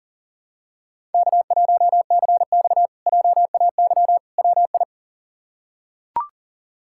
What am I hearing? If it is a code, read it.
Morse code, heard as a keyed tone, about 37 wpm, 700 Hz. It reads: K1CX JAY WI
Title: K1USN SST QSO Practice - 40wpm